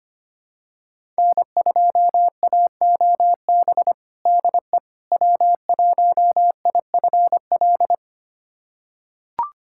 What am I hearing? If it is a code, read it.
Morse code, heard as a keyed tone, about 25 wpm, 700 Hz. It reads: N2AOB DE W1IFL
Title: USA Callsign DE Callsign - 5 Characters - 25wpm